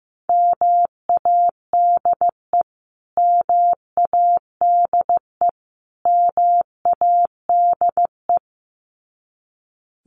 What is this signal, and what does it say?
Morse code, heard as a keyed tone, about 15 wpm, 700 Hz. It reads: MADE MADE MADE
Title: Top 500 Words Repeated 15wpm